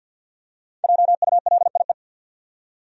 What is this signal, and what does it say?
Morse code, heard as a keyed tone, about 51 wpm, 700 Hz. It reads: JULIE
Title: USA Common Names - Max 5 Letters - 50wpm